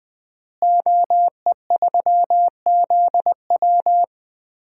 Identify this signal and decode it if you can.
Morse code, heard as a keyed tone, about 19 wpm, 700 Hz. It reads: OE3ZW
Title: DXCC-100 Call Signs - 5 Characters - Repeated 3x - 20wpm